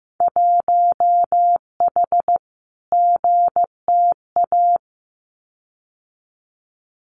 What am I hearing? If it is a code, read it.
Morse code, heard as a keyed tone, about 15 wpm, 700 Hz. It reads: 1H GTA
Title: Winter Field Day Exchanges 15wpm